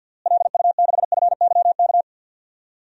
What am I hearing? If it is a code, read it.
Morse code, heard as a keyed tone, about 52 wpm, 700 Hz. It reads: LU6FYK
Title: DXCC-100 Call Signs - 6 Characters - Repeated 3x - 50wpm